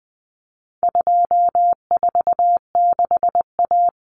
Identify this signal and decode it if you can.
Morse code, heard as a keyed tone, about 20 wpm, 700 Hz. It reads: 246A
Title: Common Cut Numbers - 4-Digit 20wpm